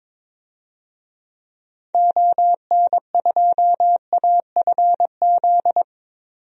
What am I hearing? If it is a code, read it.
Morse code, heard as a keyed tone, about 21 wpm, 700 Hz. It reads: ON2AFZ
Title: DXCC-100 Call Signs - 6 Characters - 22wpm